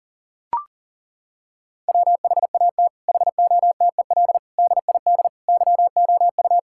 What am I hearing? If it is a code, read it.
Morse code, heard as a keyed tone, about 40 wpm, 700 Hz. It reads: WHAT HOTEL DID YOU
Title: Sentences from Top 1600 Words 40wpm